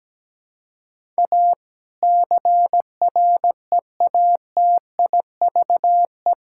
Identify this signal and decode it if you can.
Morse code, heard as a keyed tone, about 17 wpm, 700 Hz. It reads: A CREATIVE
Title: Sentences from Top 1000 Words - Review Part 5 - 17wpm